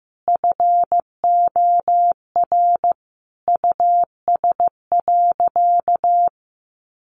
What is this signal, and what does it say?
Morse code, heard as a keyed tone, about 15 wpm, 700 Hz. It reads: FOR US.
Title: Sentences from Top 1200 Words 15wpm